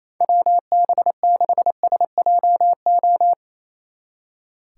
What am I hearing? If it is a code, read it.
Morse code, heard as a keyed tone, about 34 wpm, 700 Hz. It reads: WB6SJO
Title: Speed Racing - USA Call Signs 28wpm